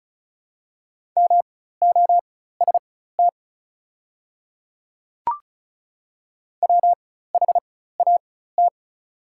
Morse code, heard as a keyed tone, about 36 wpm, spaced as for 16 wpm, 700 Hz. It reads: MOST WHAT
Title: 4x Character Spacing - Top 200 Words - 35wpm